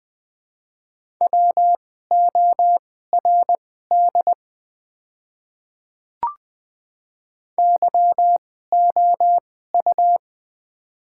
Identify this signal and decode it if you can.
Morse code, heard as a keyed tone, about 20 wpm, spaced as for 14 wpm, 690 Hz. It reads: WORD YOU
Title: 2x Character Spacing - Top 100 Words - 20wpm